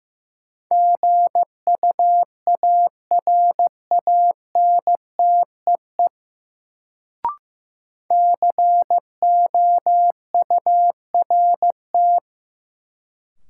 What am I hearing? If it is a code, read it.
Morse code, heard as a keyed tone, about 15 wpm, 700 Hz. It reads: GUARANTEE COURT
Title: Sentences from Top 1600 Words - New Word Drill - 15wpm